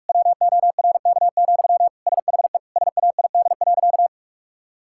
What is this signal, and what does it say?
Morse code, heard as a keyed tone, about 46 wpm, 700 Hz. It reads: WORK, SHE SAID.
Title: Sentences from Top 1000 Words - Review Part 6 - 45wpm